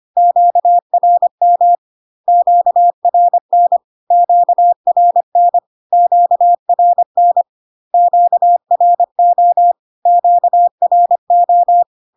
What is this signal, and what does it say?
Morse code, heard as a keyed tone, about 25 wpm, 700 Hz. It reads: QRM QRN QRN QRN QRO QRO
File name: Q-codes most common repeated 3 times 25 25wpm